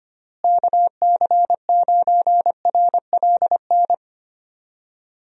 Morse code, heard as a keyed tone, about 25 wpm, 700 Hz. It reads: KC9RLN
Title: USA Call Signs - 6 Characters - Repeated 2x - 25wpm